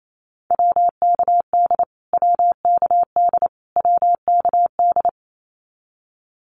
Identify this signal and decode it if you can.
Morse code, heard as a keyed tone, about 28 wpm, 700 Hz. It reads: WKD WKD WKD
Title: Abbreviations Repeated 3x - 28wpm